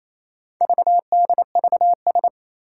Morse code, heard as a keyed tone, about 28 wpm, 700 Hz. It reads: VDVS
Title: Four Challenging Letter-Number Combinations 28wpm